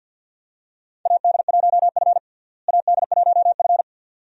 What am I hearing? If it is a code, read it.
Morse code, heard as a keyed tone, about 50 wpm, 700 Hz. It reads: AD1F AD1F
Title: USA Call Signs - 4 Characters - Repeated 2x - 50wpm